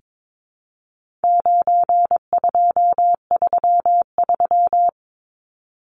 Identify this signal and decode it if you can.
Morse code, heard as a keyed tone, about 22 wpm, 700 Hz. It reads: 9233
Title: Numbers - 4-Digit 22wpm